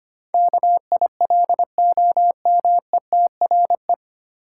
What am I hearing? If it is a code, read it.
Morse code, heard as a keyed tone, about 25 wpm, 700 Hz. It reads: KILOMETRE
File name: practice / 25wpm / 9 Letter Words - First 3 Letters Given - 25wpm